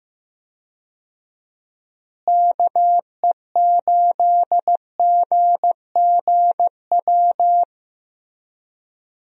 Morse code, heard as a keyed tone, about 15 wpm, 700 Hz. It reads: KE8GGW
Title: USA Call Signs - 6 Characters - Repeated 3x - 15wpm